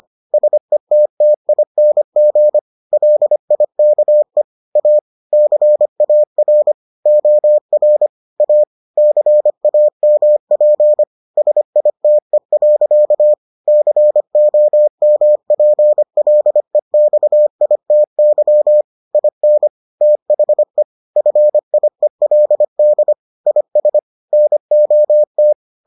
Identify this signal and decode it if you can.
Morse code, heard as a keyed tone, about 25 wpm, 600 Hz. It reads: SETTING LIKE A CAR OR A CAMP SITE. COMPLEXITY IN THE FIELD IS NOT